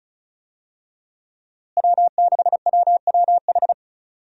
Morse code, heard as a keyed tone, about 36 wpm, 700 Hz. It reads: W6WWH
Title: USA Call Signs - 5 Characters - Repeated 3x - 35wpm